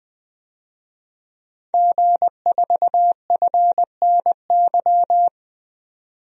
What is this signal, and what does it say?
Morse code, heard as a keyed tone, about 20 wpm, 700 Hz. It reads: G4FNY
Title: DXCC-100 Call Signs - 5 Characters - 20wpm